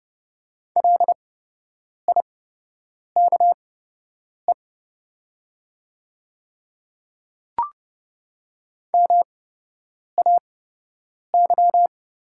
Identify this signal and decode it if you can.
Morse code, heard as a keyed tone, about 30 wpm, spaced as for 8 wpm, 700 Hz. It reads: LIKE MAY
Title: 8x Character Spacing - Top 200 Words - 30wpm